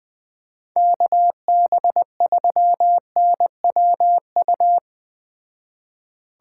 Morse code, heard as a keyed tone, about 20 wpm, 700 Hz. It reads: KB3NWU